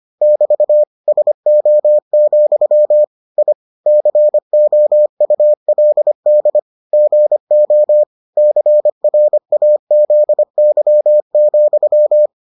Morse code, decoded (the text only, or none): <BT> SO, I COULD GO CRAZY,